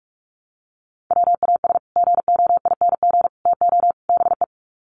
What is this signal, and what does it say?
Morse code, heard as a keyed tone, about 44 wpm, 700 Hz. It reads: WAS GOING TO BE
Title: Sets of 4 Words 45wpm